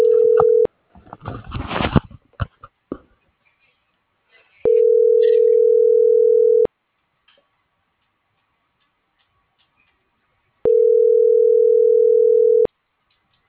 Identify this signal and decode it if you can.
Ringback tone (North America), 440 + 480 Hz.